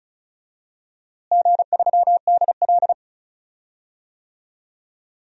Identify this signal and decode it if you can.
Morse code, heard as a keyed tone, about 36 wpm, 700 Hz. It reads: G3DL
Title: DXCC-100 Call Signs - 4 Characters - Repeated 3x - 35wpm